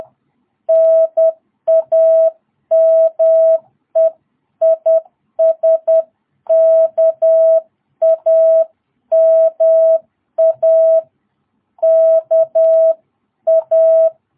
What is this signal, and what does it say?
Morse code, heard as a keyed tone, about 10 wpm, 660 Hz. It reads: NAME IS KAMA KA
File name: jh8cbh / 2022cw